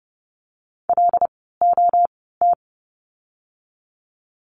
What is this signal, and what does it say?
Morse code, heard as a keyed tone, about 30 wpm, spaced as for 17 wpm, 700 Hz. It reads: LOT
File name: course / 30:10wpm / 108 Lesson 18f - Letter m - Words 30wpm